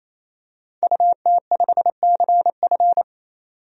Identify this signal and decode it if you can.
Morse code, heard as a keyed tone, about 28 wpm, 700 Hz. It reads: UT5CF
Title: DXCC-100 Call Signs - 5 Characters - 28wpm